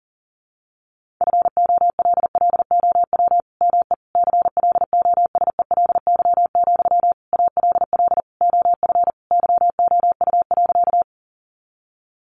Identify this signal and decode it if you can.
Morse code, heard as a keyed tone, about 37 wpm, 700 Hz. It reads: FOLLOW ME CLOSELY, ALL OF YOU.
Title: Sentences from Top 1400 Words 40wpm